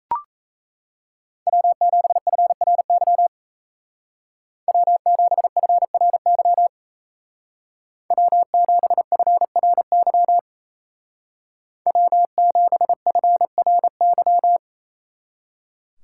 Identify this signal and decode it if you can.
Morse code, heard as a keyed tone, about 35 wpm, 700 Hz. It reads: W7FRY W7FRY W7FRY W7FRY
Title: Speed Racing - USA 1x3 Call Signs 28wpm